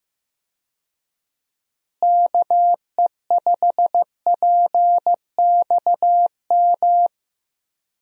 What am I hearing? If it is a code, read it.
Morse code, heard as a keyed tone, about 15 wpm, 700 Hz. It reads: KE5PXM
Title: USA Call Signs - 6 Characters - Repeated 3x - 15wpm